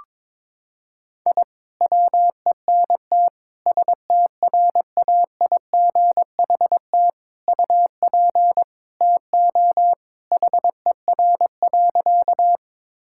Morse code, heard as a keyed tone, about 22 wpm, 700 Hz. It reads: I WENT STRAIGHT UP TO HER.
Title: Sentences from Top 1000 Words - Review Part 6 - 22wpm